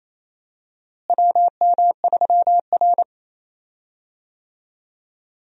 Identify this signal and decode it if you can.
Morse code, heard as a keyed tone, about 27 wpm, 700 Hz. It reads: WM3R
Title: USA Call Signs - 4 Characters - Repeated 3x - 28wpm